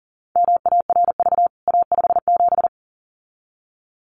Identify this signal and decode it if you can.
Morse code, heard as a keyed tone, about 40 wpm, 700 Hz. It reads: MARV A57
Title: CWT Exchange 40wpm